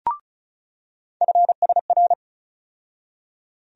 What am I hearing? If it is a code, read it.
Morse code, heard as a keyed tone, about 36 wpm, 700 Hz. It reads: FSR
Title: USA Call Sign Postfixes - 3 Letters 35wpm